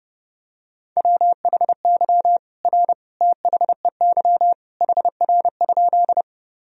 Morse code, heard as a keyed tone, about 30 wpm, 700 Hz. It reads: WHY R THEY HR?
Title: Sentences from Top 200 Words and Abbreviations 30wpm